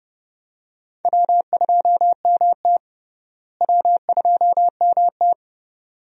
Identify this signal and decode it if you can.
Morse code, heard as a keyed tone, about 31 wpm, 700 Hz. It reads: W2MT W2MT
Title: USA Call Signs - 4 Characters - Repeated 2x - 30wpm